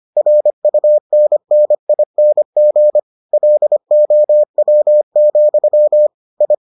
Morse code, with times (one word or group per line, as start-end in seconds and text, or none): 0.2-3.0 RUNNING
3.3-6.1 LOW,
6.4-6.5 I